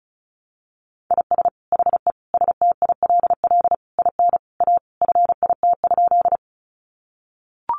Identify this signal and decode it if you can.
Morse code, heard as a keyed tone, about 34 wpm, 700 Hz. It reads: IS HE STILL IN A FIT?
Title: Sentences from Top 800 Words 35wpm